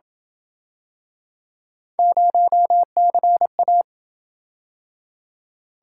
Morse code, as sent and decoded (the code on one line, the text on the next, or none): ----- -.-. .-
0CA